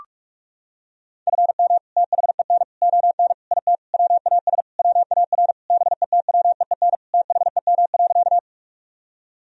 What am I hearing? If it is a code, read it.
Morse code, heard as a keyed tone, about 46 wpm, 700 Hz. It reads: FM THEN ON IT WAS WAR BETWEEN THEM.
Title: Sentences from Top 400 Words and Abbreviations 45wpm